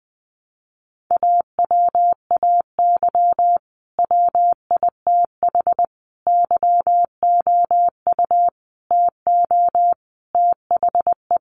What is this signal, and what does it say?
Morse code, heard as a keyed tone, about 20 wpm, 700 Hz. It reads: AWAY WITH YOU TO THE
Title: Sentences from Top 1400 Words 20wpm